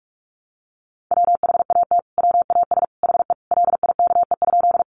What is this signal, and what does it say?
Morse code, heard as a keyed tone, about 44 wpm, 700 Hz. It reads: WHAT WAS HE LIKE?
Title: Sentences from Top 100 Words 45wpm